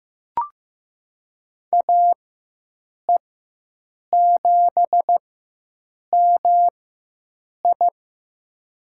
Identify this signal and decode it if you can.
Morse code, heard as a keyed tone, about 15 wpm, spaced as for 7 wpm, 700 Hz. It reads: AE7MI
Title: USA Call Signs - 5 Characters - 4x Character Spacing - 15wpm